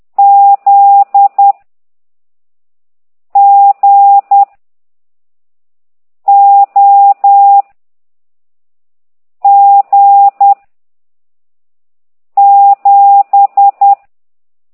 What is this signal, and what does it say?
Morse code, heard as a keyed tone, about 10 wpm, 800 Hz. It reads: Z G O G 7
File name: ct2idw / exerc15